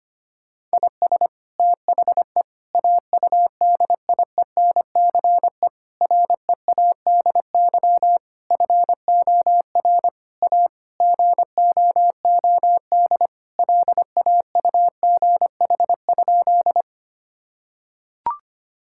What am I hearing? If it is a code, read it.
Morse code, heard as a keyed tone, about 25 wpm, 700 Hz. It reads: IS THE AUDIENCE READY FOR A GOOD LAUGH?
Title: Sentences from Top 1000 Words - Review Part 5 - 25wpm